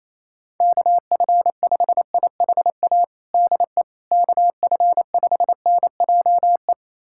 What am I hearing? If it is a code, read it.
Morse code, heard as a keyed tone, about 28 wpm, 700 Hz. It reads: KF5IHA DE KF5NJE
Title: USA Callsign DE Callsign 28wpm